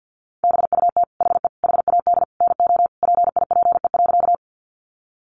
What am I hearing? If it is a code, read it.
Morse code, heard as a keyed tone, about 49 wpm, 700 Hz. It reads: BUT HE HAD NO PIPE.
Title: Sentences from Top 1500 Words 50wpm